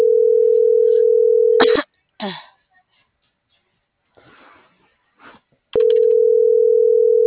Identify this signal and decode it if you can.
Ringback tone (North America), 440 + 480 Hz.